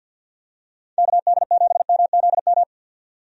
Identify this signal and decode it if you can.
Morse code, heard as a keyed tone, about 51 wpm, 700 Hz. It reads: KD7MZM
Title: USA Call Signs - 6 Characters - 50wpm